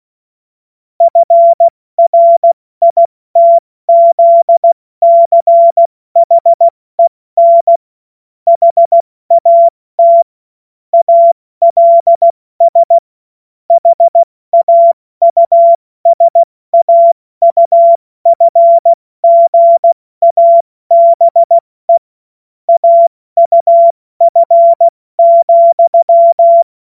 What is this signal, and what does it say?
Morse code, heard as a keyed tone, about 16 wpm, 680 Hz. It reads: FRITZCHEN HAT ALS HAUSAUFGABE AUF,
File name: Morsekurs / Witze / Witz 24. August 2021, 70 F80